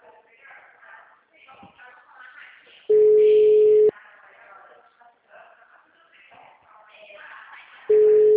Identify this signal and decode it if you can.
Ringback tone (Europe, 425 Hz).